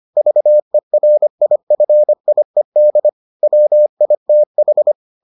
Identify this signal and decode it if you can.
Morse code, heard as a keyed tone, about 25 wpm, 600 Hz. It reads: VERIFIED WITH